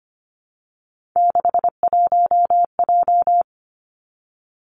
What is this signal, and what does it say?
Morse code, heard as a keyed tone, about 25 wpm, 700 Hz. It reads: 61J
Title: Three Challenging Letter-Number Combinations 25wpm